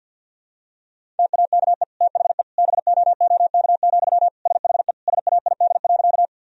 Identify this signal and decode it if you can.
Morse code, heard as a keyed tone, about 46 wpm, 700 Hz. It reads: TAKE THE BOOK, SHE SAID.
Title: Sentences from Top 1000 Words - Review Part 1 - 50wpm